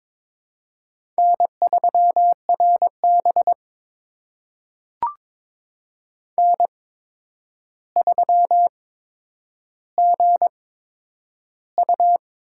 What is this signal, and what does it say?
Morse code, heard as a keyed tone, about 22 wpm, spaced as for 9 wpm, 700 Hz. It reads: N3RB N3GU
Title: USA Call Signs - 4 Characters - 8x Character Spacing - 22wpm